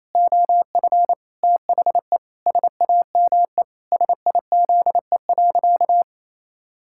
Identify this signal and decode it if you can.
Morse code, heard as a keyed tone, about 28 wpm, 700 Hz. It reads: OF THE SAME SIZE.
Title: Pangram Challenge v2 - 28wpm